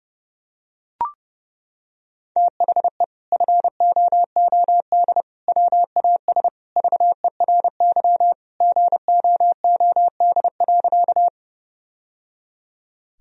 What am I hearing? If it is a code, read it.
Morse code, heard as a keyed tone, about 30 wpm, 700 Hz. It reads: THE FOOD WAS VERY GOOD.
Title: Sentences from Top 1000 Words - Review Part 1 - 30wpm